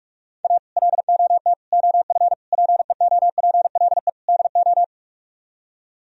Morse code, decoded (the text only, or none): A LOT OF PEOPLE DO